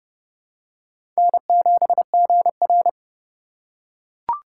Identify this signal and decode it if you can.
Morse code, heard as a keyed tone, about 30 wpm, 700 Hz. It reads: N7GR
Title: USA Call Signs - 4 Characters - 4x Character Spacing - 30wpm